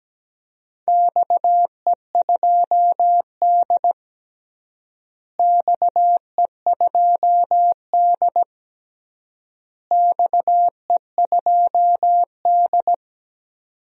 Morse code, heard as a keyed tone, about 17 wpm, 700 Hz. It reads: XE2D XE2D XE2D
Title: DXCC-100 Call Signs - 4 Characters - Repeated 3x - 17wpm